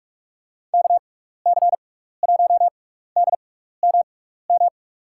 Morse code, heard as a keyed tone, about 46 wpm, spaced as for 15 wpm, 700 Hz. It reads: KC1DMM
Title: USA Call Signs - 6 Characters - 6x Character Spacing - 45wpm